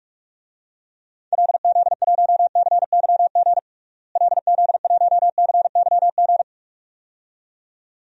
Morse code, heard as a keyed tone, about 47 wpm, 690 Hz. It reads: LZ1CYG LZ1CYG
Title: DXCC-100 Call Signs - 6 Characters - Repeated 2x - 45wpm